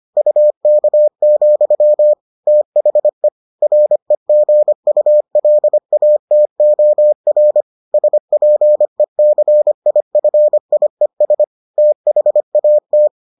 Morse code, heard as a keyed tone, about 25 wpm, 600 Hz. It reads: UK, THE REGULATOR SPECIFIES THAT